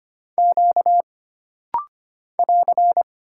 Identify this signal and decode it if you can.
Morse code, heard as a keyed tone, about 25 wpm, 700 Hz. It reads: Q <AR>
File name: Single Character (Letter:Number:Punctuation:Prosign) - Rapid-Fire - 25wpm